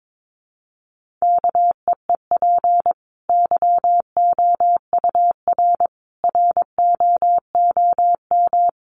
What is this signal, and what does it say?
Morse code, heard as a keyed tone, about 22 wpm, 700 Hz. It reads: KEEP YOUR ROOM
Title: Sentences from Top 1100 Words 22wpm